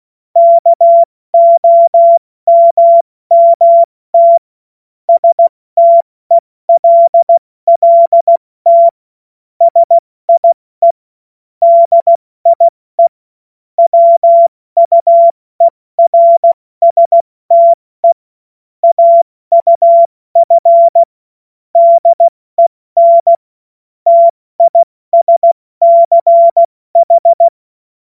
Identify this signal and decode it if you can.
Morse code, heard as a keyed tone, about 16 wpm, 680 Hz. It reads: KOMMT STELLT SIE DIE WUERSTE AUF DEN TISCH
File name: Morsekurs / Witze / Witz 09 April 2024 70 F80